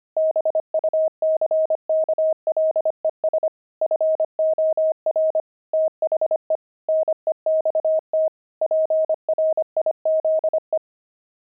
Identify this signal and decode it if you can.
Morse code, heard as a keyed tone, about 25 wpm, 630 Hz. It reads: BUCKLES FOR THE NEXT PRIZE
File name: pangram-fs-25-cs-25-buckles